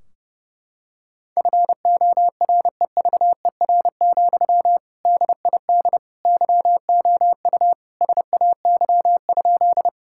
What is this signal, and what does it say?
Morse code, heard as a keyed tone, about 30 wpm, 700 Hz. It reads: FOREVER, DID YOU SAY?
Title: Sentences from Top 1600 Words 30wpm